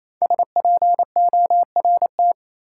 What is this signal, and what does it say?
Morse code, heard as a keyed tone, about 27 wpm, 700 Hz. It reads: SPORT
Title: Sentences from Top 1500 Words - New Word Drill Alt - 28wpm